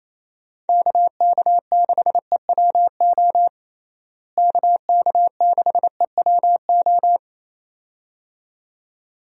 Morse code, heard as a keyed tone, about 27 wpm, 700 Hz. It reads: KK6EWO KK6EWO
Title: USA Call Signs - 6 Characters - Repeated 2x - 28wpm